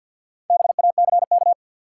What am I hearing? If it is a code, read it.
Morse code, heard as a keyed tone, about 50 wpm, 700 Hz. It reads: BACK
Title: Sentences from Top 200 Words - New Word Drill - 50wpm